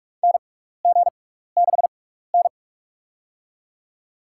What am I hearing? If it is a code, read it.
Morse code, heard as a keyed tone, about 46 wpm, spaced as for 16 wpm, 700 Hz. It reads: NG6N
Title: USA Call Signs - 4 Characters - 6x Character Spacing - 45wpm